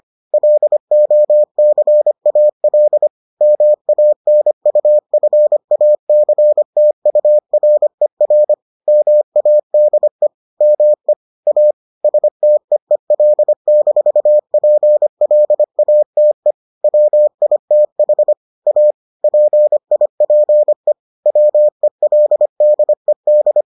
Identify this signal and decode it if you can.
Morse code, heard as a keyed tone, about 25 wpm, 600 Hz. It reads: LOCAL MANUFACTURER MADE ME A STEEL-PLATE WITH A PIPE WELDED